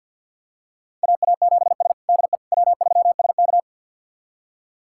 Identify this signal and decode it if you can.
Morse code, heard as a keyed tone, about 52 wpm, 700 Hz. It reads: AA7S DE W3SK
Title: USA Callsign DE Callsign - 4 Characters - 50wpm